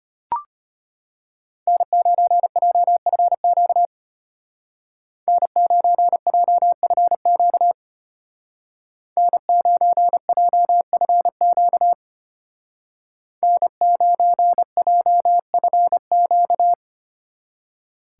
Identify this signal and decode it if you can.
Morse code, heard as a keyed tone, about 32 wpm, 700 Hz. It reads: N9JFQ N9JFQ N9JFQ N9JFQ